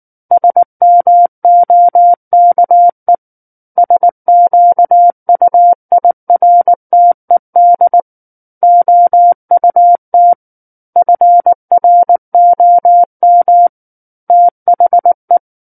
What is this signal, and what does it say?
Morse code, heard as a keyed tone, about 19 wpm, 700 Hz. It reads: SMOKE SQUIRTED OUT FROM THE
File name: Book One / 10-Chapter 10 - 19 WPM